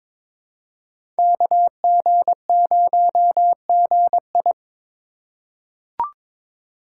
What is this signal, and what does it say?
Morse code, heard as a keyed tone, about 22 wpm, 700 Hz. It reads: KG0GI
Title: USA Call Signs - 5 Characters - 3x Character Spacing - 22wpm